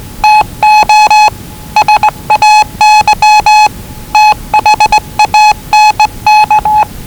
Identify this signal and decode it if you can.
Morse code, heard as a keyed tone, about 19 wpm, 840 Hz. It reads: TO SAY THANK
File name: WellingtonRadioZlwCloseDownOnHfAndMf / 01-Zlw 4277khz 417.5 qrt until 0925z